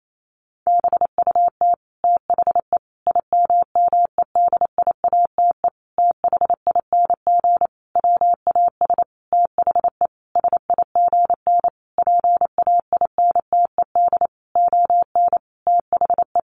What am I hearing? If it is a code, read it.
Morse code, heard as a keyed tone, about 28 wpm, 700 Hz. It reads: BUT THE IMMEDIATE THING WAS THE SIGN PAINTED ON THE
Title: A Sound Of Thunder 28wpm